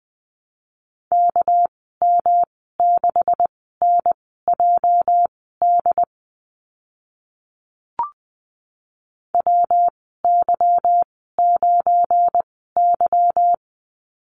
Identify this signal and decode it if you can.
Morse code, heard as a keyed tone, about 20 wpm, spaced as for 14 wpm, 700 Hz. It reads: KM6NJD WY9Y
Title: 219 Lesson 35g - Number 0 - Call Signs 20wpm